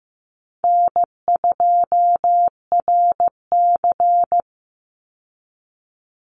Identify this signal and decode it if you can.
Morse code, heard as a keyed tone, about 15 wpm, 700 Hz. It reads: N2RC
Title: Easy Call Signs Repeated 3x - 15wpm